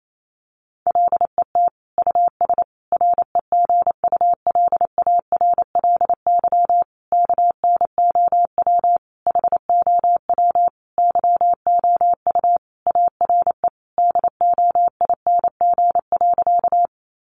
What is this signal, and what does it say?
Morse code, heard as a keyed tone, about 28 wpm, 700 Hz. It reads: LET US REGULARLY KNOW HOW YOU ARE DOING.
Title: Sentences from Top 1400 Words 28wpm